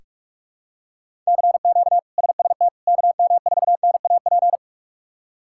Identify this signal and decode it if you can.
Morse code, heard as a keyed tone, about 46 wpm, 700 Hz. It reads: CQ SST KM4NAP
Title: K1USN SST QSO Practice - 45wpm